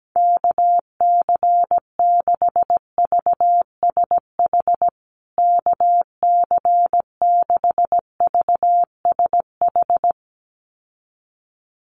Morse code, heard as a keyed tone, about 17 wpm, 700 Hz. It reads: KC6VSH KC6VSH
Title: USA Call Signs Repeated 2x 17wpm